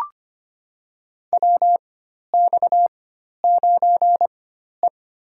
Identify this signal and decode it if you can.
Morse code, heard as a keyed tone, about 25 wpm, spaced as for 12 wpm, 700 Hz. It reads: WX9E
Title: USA Call Signs - 4 Characters - 4x Character Spacing - 25wpm